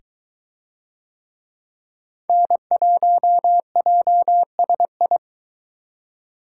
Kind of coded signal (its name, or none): Morse code